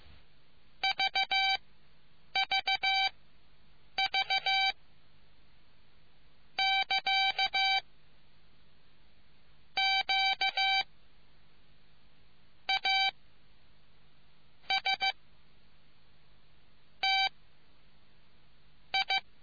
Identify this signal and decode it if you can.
Morse code, heard as a keyed tone, about 15 wpm, 770 Hz. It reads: V V V <KA> Q A S T I